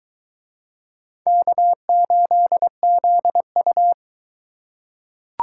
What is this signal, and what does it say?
Morse code, heard as a keyed tone, about 21 wpm, 700 Hz. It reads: K8ZU